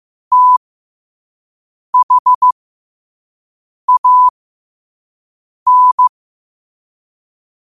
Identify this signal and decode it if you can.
Morse code, heard as a keyed tone, about 15 wpm, spaced as for 5 wpm, 1000 Hz. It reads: THAN
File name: qotdfiles / 5wpm